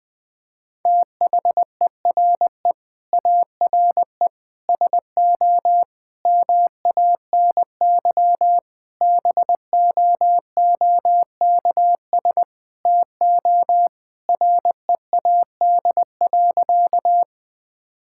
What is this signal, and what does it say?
Morse code, heard as a keyed tone, about 20 wpm, 700 Hz. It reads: THERE ARE SO MANY BOOKS TO READ.
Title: Sentences with 4-Word Phrases - 20wpm